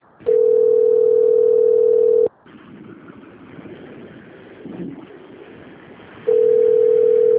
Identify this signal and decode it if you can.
Ringback tone (North America), 440 + 480 Hz.